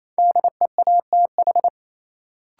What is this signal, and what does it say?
Morse code, heard as a keyed tone, about 28 wpm, 700 Hz. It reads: DEATH
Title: Sentences from Top 1100 Words - New Word Drill Alt - 28wpm